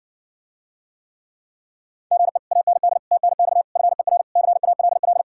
Morse code, heard as a keyed tone, about 60 wpm, 700 Hz. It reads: GE AND TNX FER CALL